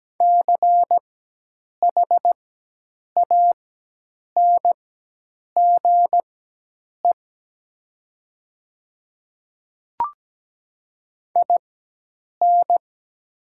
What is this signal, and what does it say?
Morse code, heard as a keyed tone, about 17 wpm, spaced as for 8 wpm, 700 Hz. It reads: CHANGE IN
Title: 4x Character Spacing - Top 300 Words - 17wpm